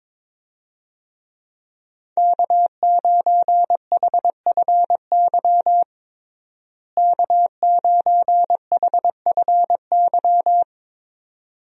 Morse code, heard as a keyed tone, about 22 wpm, 700 Hz. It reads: K9HFY K9HFY
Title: USA Call Signs - 5 Characters - Repeated 2x - 22wpm